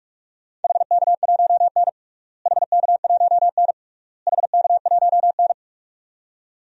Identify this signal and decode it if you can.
Morse code, heard as a keyed tone, about 46 wpm, 700 Hz. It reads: HK1N HK1N HK1N
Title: DXCC-100 Call Signs - 4 Characters - Repeated 3x - 45wpm